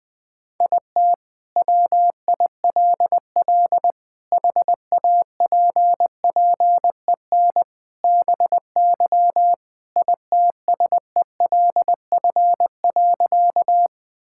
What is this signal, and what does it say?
Morse code, heard as a keyed tone, about 21 wpm, 700 Hz. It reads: IT WILL HAPPEN BY ITSELF.
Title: Sentences from Top 1000 Words - Review Part 5 - 20wpm